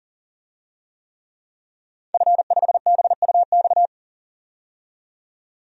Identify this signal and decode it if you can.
Morse code, heard as a keyed tone, about 40 wpm, 700 Hz. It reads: F5BUX
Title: DXCC-100 Call Signs - 5 Characters - 40wpm